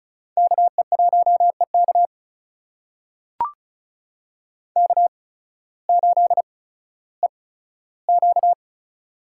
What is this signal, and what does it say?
Morse code, heard as a keyed tone, about 34 wpm, spaced as for 16 wpm, 700 Hz. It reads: KE1EK K8EQ
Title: USA Call Signs - 5 Characters - 8x Character Spacing - 35wpm